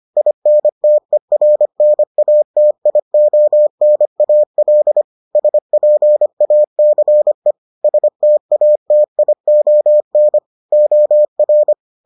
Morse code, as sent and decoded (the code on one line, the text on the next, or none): .. -. - . .-. -. .- - .. --- -. .- .-.. / ... .--. .- -.-. / ... - .- - .. --- -. / --- .-.
INTERNATIONAL SPACE STATION OR